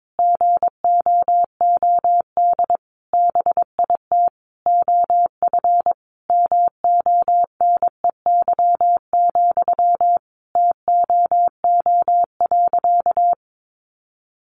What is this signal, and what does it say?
Morse code, heard as a keyed tone, about 22 wpm, 700 Hz. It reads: GOOD BIT OF MONEY, TOO.
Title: Sentences from Top 900 Words 22wpm